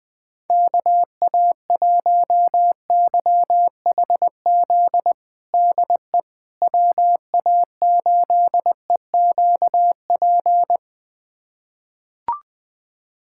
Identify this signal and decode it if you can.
Morse code, heard as a keyed tone, about 20 wpm, 700 Hz. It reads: KA1YHZ DE WA8EQP
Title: USA Callsign DE Callsign - 6 Characters - 20wpm